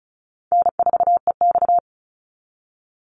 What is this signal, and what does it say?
Morse code, heard as a keyed tone, about 34 wpm, 700 Hz. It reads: N4EX
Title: NA SOTA Chaser Callsigns 35wpm